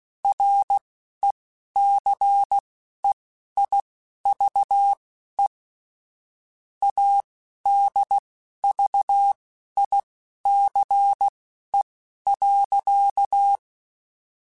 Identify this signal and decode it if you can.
Morse code, heard as a keyed tone, about 16 wpm, spaced as for 11 wpm, 800 Hz. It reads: RECEIVE ADVICE.